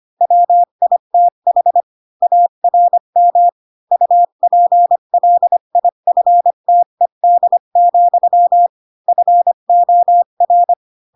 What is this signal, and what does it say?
Morse code, heard as a keyed tone, about 25 wpm, 700 Hz. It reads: WITH ARM UPLIFTED, FOR
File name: Pinocchio / Pinocchio ch1 20 25 ik1hge